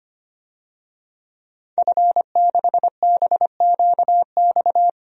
Morse code, heard as a keyed tone, about 25 wpm, 700 Hz. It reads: F6BQX
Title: DXCC-100 Call Signs - 5 Characters - 25wpm